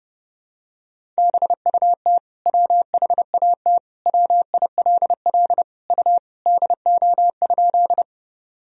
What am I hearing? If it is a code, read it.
Morse code, heard as a keyed tone, about 30 wpm, 700 Hz. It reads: BUT WHAT WILL U DO?
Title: Sentences from Top 100 Words and Abbreviations 30wpm